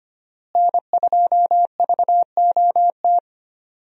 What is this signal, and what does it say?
Morse code, heard as a keyed tone, about 25 wpm, 700 Hz. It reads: N2VOT
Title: USA Call Signs - 5 Characters - Repeated 3x - 25wpm